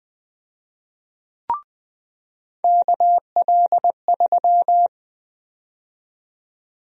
Morse code, heard as a keyed tone, about 20 wpm, 700 Hz. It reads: KL3
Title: USA Call Sign Prefixes - 20wpm